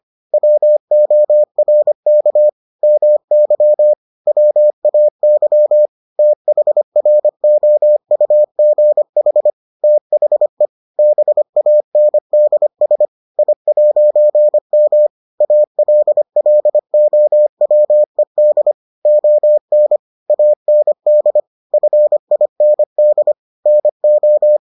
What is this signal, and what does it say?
Morse code, heard as a keyed tone, about 25 wpm, 600 Hz. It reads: WORK MY WAY THROUGH THE BANDS I'M ALLOWED ON AND FIND NO